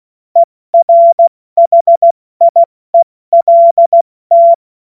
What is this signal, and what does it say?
Morse code, heard as a keyed tone, about 16 wpm, 680 Hz. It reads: ERHIELT
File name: Morsekurs / Witze / Witz 04. Oktober 2022 70 F80